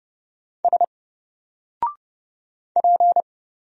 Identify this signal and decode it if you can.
Morse code, heard as a keyed tone, about 30 wpm, 700 Hz. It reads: S P
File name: Single-Letter-Number-30wpm